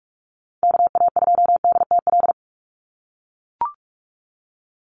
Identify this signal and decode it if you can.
Morse code, heard as a keyed tone, about 44 wpm, 700 Hz. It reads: KA2DTL
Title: USA Call Signs 45wpm